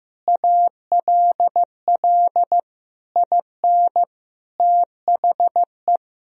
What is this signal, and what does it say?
Morse code, heard as a keyed tone, about 15 wpm, 700 Hz. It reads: ALL IN THE
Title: Sentences from Top 1000 Words - Review Part 3 - 15wpm